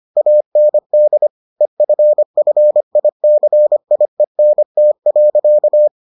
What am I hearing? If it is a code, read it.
Morse code, heard as a keyed tone, about 25 wpm, 600 Hz. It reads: AND EFFICIENT.